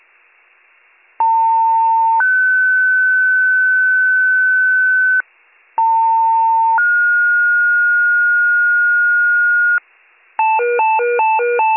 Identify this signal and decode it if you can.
Two-tone page, heard as a one-second tone, then a three-second tone, about 900 then 1500 Hz, about 900 then 1430 Hz. It is heard from 1.2 to 9.8 s.